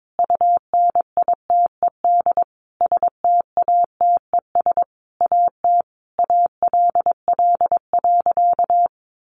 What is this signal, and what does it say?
Morse code, heard as a keyed tone, about 22 wpm, 700 Hz. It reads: UNITED STATES AT ALL.
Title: A Sound Of Thunder 22wpm